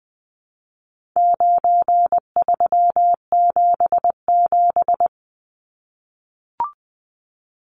Morse code, heard as a keyed tone, about 20 wpm, 700 Hz. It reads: 9377
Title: Numbers - 4-Digit 20wpm